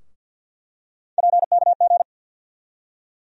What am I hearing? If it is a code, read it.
Morse code, heard as a keyed tone, about 48 wpm, 700 Hz. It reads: PKG
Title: USA Call Sign Postfixes - 3 Letters 50wpm